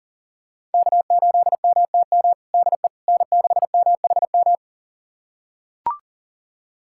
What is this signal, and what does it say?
Morse code, heard as a keyed tone, about 39 wpm, 700 Hz. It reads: K8MTM DE N6MHM
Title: USA Callsign DE Callsign - 5 Characters - 40wpm